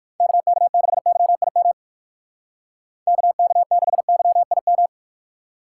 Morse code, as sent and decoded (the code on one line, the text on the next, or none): -.- -.- -.... -.-- .. -- / -.- -.- -.... -.-- .. --
KK6YIM KK6YIM